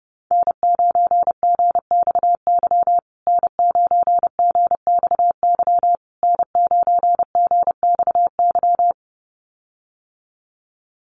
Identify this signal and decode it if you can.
Morse code, heard as a keyed tone, about 30 wpm, 700 Hz. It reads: N9GXY N9GXY N9GXY
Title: USA Call Signs Repeated 3x 30wpm